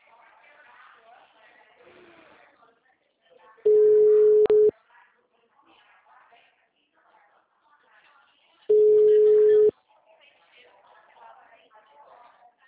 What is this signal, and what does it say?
Ringback tone (Europe, 425 Hz), with speech.
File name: out- -2012- - - .55428